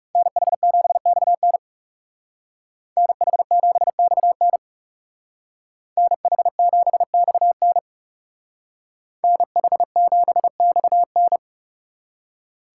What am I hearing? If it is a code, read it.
Morse code, heard as a keyed tone, about 38 wpm, 700 Hz. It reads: NH7XN NH7XN NH7XN NH7XN